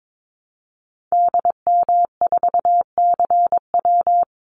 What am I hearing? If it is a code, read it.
Morse code, heard as a keyed tone, about 22 wpm, 700 Hz. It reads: DM4CW
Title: Call Signs - 22wpm